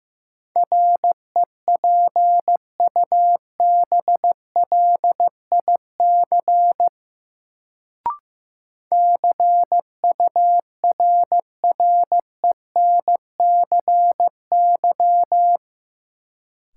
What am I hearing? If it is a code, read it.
Morse code, heard as a keyed tone, about 15 wpm, 700 Hz. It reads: REPUBLIC CURRENCY
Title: Sentences from Top 1500 Words - New Word Drill Alt - 15wpm